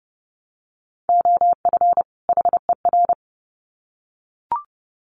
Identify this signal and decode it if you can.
Morse code, heard as a keyed tone, about 30 wpm, 700 Hz. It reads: OF HER
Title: Sets of 2 Words 30wpm